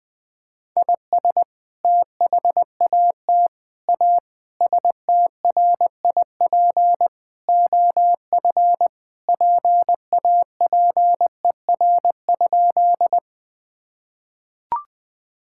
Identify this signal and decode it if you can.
Morse code, heard as a keyed tone, about 21 wpm, 700 Hz. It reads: IS THAT A STRIP OF PAPER?
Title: Sentences from Top 1600 Words 20wpm